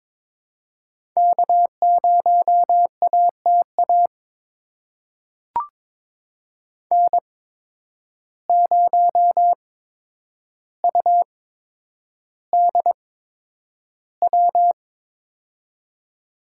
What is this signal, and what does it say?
Morse code, heard as a keyed tone, about 22 wpm, spaced as for 9 wpm, 700 Hz. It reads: K0ATA N0UDW
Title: USA Call Signs - 5 Characters - 8x Character Spacing - 22wpm